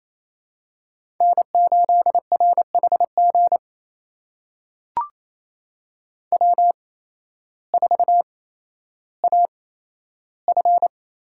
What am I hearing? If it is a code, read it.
Morse code, heard as a keyed tone, about 28 wpm, spaced as for 13 wpm, 700 Hz. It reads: N8RHG W4AF
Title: USA Call Signs - 5 Characters - 8x Character Spacing - 28wpm